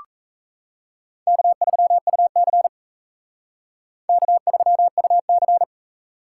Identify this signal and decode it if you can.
Morse code, heard as a keyed tone, about 40 wpm, 700 Hz. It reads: K3UC K3UC